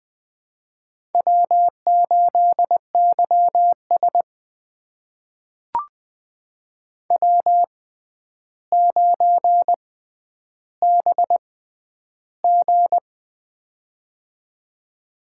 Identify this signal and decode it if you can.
Morse code, heard as a keyed tone, about 20 wpm, spaced as for 10 wpm, 700 Hz. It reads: W8YS W9BG
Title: USA Call Signs - 4 Characters - 6x Character Spacing - 20wpm